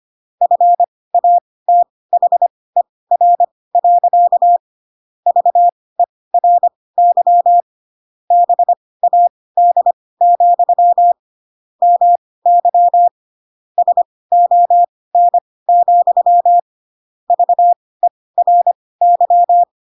Morse code, heard as a keyed tone, about 25 wpm, spaced as for 18 wpm, 700 Hz. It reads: FATHER. VERY BAD, MY SON, VERY
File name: Pinocchio ch3 12 25 ik1hge